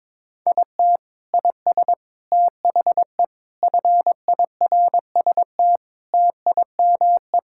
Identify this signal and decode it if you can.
Morse code, heard as a keyed tone, about 21 wpm, 700 Hz. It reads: IT IS THE FIRST TIME
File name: practice / 22wpm / Sentences with 4-Word Phrases - 22wpm